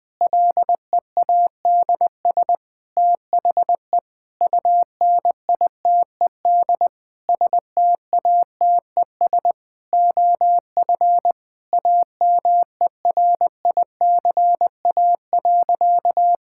Morse code, heard as a keyed tone, about 20 wpm, 700 Hz. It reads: LEADS THE UNITED STATES OF AMERICA.
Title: Sentences with 5-Word Phrases - 20wpm